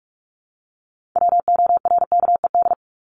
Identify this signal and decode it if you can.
Morse code, heard as a keyed tone, about 44 wpm, 700 Hz. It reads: WORKED
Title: ED Suffix - Top 100 Words with Base Words 45wpm